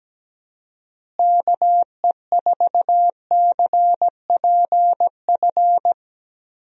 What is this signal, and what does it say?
Morse code, heard as a keyed tone, about 17 wpm, 700 Hz. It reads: KE4CPF
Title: USA Call Signs - 6 Characters - Repeated 2x - 17wpm